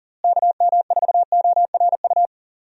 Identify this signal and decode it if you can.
Morse code, heard as a keyed tone, about 37 wpm, 700 Hz. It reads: KM4ORU
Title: USA Call Signs - 6 Characters - Repeated 2x - 40wpm